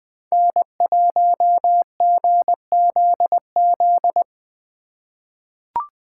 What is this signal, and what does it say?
Morse code, heard as a keyed tone, about 20 wpm, 700 Hz. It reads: N1GZZ
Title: USA Call Signs - 5 Characters - 2x Character Spacing - 20wpm